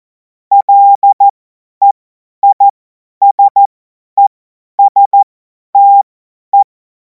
Morse code, heard as a keyed tone, about 14 wpm, spaced as for 10 wpm, 800 Hz. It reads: LEISESTE
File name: Morsekurs / Witze / Witz 02. Juni 2026 50 F70